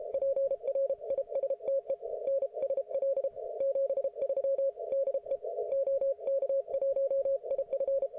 Morse code, heard as a keyed tone, about 33 wpm, 550 Hz. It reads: PRISTE NSL 73 DE OK1IF